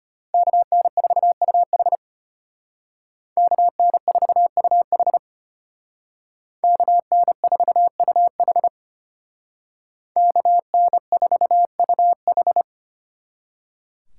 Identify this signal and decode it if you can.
Morse code, heard as a keyed tone, about 33 wpm, 700 Hz. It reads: KN4UH KN4UH KN4UH KN4UH